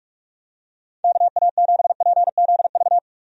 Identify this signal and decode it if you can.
Morse code, heard as a keyed tone, about 47 wpm, 700 Hz. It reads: KA7PZV
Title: USA Call Signs - 6 Characters - 45wpm